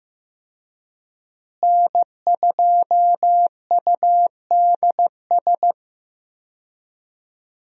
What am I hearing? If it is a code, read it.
Morse code, heard as a keyed tone, about 15 wpm, 700 Hz. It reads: N2UDS
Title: USA Call Signs - 5 Characters - Repeated 2x - 15wpm